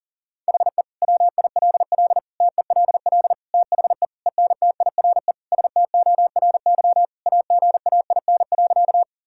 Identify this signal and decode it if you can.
Morse code, heard as a keyed tone, about 40 wpm, 700 Hz. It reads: HE WILL TELL THE ENTIRE STORY AGAIN.
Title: Sentences from Top 1000 Words - Review Part 5 - 40wpm